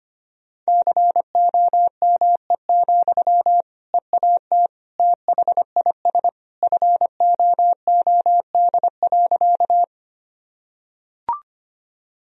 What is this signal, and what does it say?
Morse code, heard as a keyed tone, about 25 wpm, 700 Hz. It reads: COME, EAT THIS FOOD.
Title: Sentences from Top 1000 Words - Review Part 1 - 25wpm